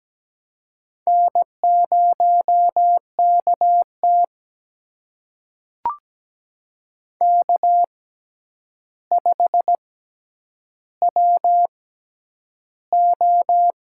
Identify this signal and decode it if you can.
Morse code, heard as a keyed tone, about 17 wpm, spaced as for 9 wpm, 700 Hz. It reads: N0KT K5WO
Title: USA Call Signs - 4 Characters - 6x Character Spacing - 17wpm